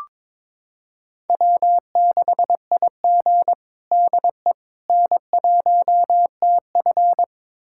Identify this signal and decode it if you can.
Morse code, heard as a keyed tone, about 21 wpm, 700 Hz. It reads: W6IG DE N1TF
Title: USA Callsign DE Callsign - 4 Characters - 22wpm